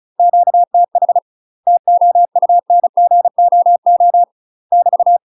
Morse code, heard as a keyed tone, about 36 wpm, 700 Hz. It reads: QTH TOUNGOO <BT>
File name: UK MASTER QSOs TRAINING / 35WPM / UK MASTER QSOs 0044